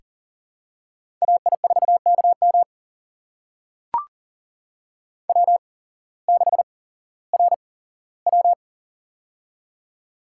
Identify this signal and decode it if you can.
Morse code, heard as a keyed tone, about 40 wpm, spaced as for 19 wpm, 700 Hz. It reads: AI4KM W6RW
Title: USA Call Signs - 8x Character Spacing - 40wpm